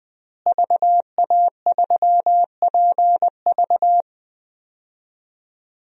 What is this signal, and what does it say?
Morse code, heard as a keyed tone, about 20 wpm, 700 Hz. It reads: VA3PV
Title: DXCC-100 Call Signs - 5 Characters - Repeated 3x - 20wpm